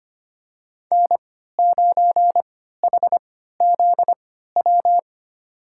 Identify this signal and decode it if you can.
Morse code, heard as a keyed tone, about 25 wpm, spaced as for 14 wpm, 700 Hz. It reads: N9HZW
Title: USA Call Signs - 5 Characters - 3x Character Spacing - 25wpm